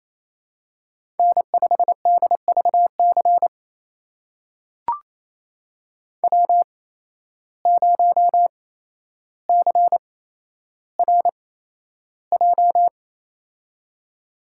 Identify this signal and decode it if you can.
Morse code, heard as a keyed tone, about 27 wpm, spaced as for 12 wpm, 700 Hz. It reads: N5DVC W0CRJ
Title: USA Call Signs - 5 Characters - 8x Character Spacing - 28wpm